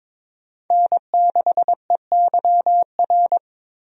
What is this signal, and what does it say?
Morse code, heard as a keyed tone, about 21 wpm, 700 Hz. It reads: N6EYR
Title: USA Call Signs - 5 Characters - Repeated 2x - 22wpm